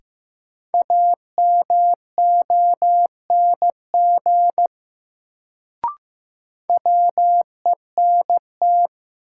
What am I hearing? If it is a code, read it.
Morse code, heard as a keyed tone, about 15 wpm, 700 Hz. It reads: AMONG WENT
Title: Sets of 2 Words - New Word Drill - 15wpm